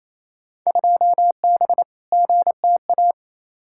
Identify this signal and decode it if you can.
Morse code, heard as a keyed tone, about 28 wpm, 700 Hz. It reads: 2B GTA
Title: Field Day Exchanges Encore - 28wpm